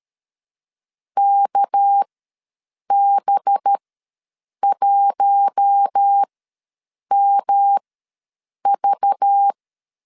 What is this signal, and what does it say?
Morse code, heard as a keyed tone, about 13 wpm, spaced as for 7 wpm, 800 Hz. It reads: KB1MV
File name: LEZIONI / 7 GF9Z / 7 G ws 12wpm 3140